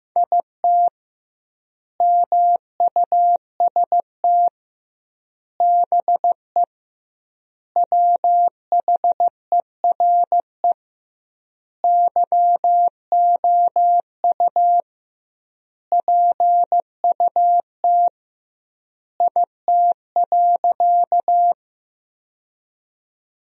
Morse code, heard as a keyed tone, about 15 wpm, 700 Hz. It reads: IT MUST BE WHERE YOU PUT IT.
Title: 2x Word-Spacing - Sentences from Top 200 Words - 15wpm